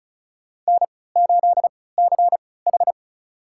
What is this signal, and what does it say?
Morse code, heard as a keyed tone, about 36 wpm, spaced as for 20 wpm, 700 Hz. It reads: N8CH
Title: USA Call Signs - 4 Characters - 3x Character Spacing - 35wpm